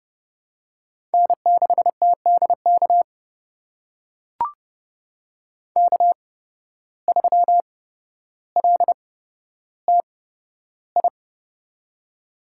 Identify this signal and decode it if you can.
Morse code, heard as a keyed tone, about 30 wpm, spaced as for 13 wpm, 700 Hz. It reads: N6TDK K3LTI
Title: USA Call Signs - 5 Characters - 8x Character Spacing - 30wpm